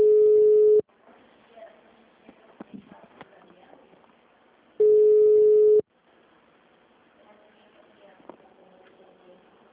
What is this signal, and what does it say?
Ringback tone (Europe, 425 Hz).
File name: out- -2012- - - .12647